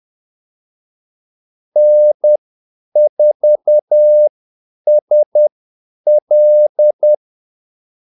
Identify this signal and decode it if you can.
Morse code, heard as a keyed tone, about 10 wpm, spaced as for 8 wpm, 600 Hz. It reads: N4SL
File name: Callsigns-10-8wpm